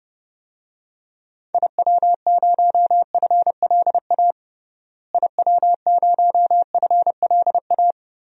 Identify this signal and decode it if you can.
Morse code, heard as a keyed tone, about 30 wpm, 700 Hz. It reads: IW0FLA IW0FLA
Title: DXCC-100 Call Signs - 6 Characters - Repeated 3x - 30wpm